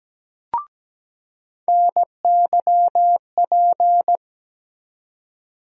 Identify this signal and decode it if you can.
Morse code, heard as a keyed tone, about 17 wpm, 700 Hz. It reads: NYP
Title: USA Call Sign Postfixes - 3 Letters 17wpm